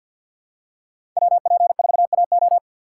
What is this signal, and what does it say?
Morse code, heard as a keyed tone, about 54 wpm, 700 Hz. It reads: WP4AO
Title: USA Call Signs - 5 Characters - 8x Character Spacing - 50wpm